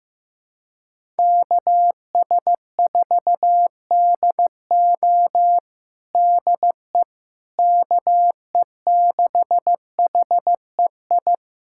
Morse code, heard as a keyed tone, about 15 wpm, 700 Hz. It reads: KS4DO DE KE6HEI
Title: USA Callsign DE Callsign 15wpm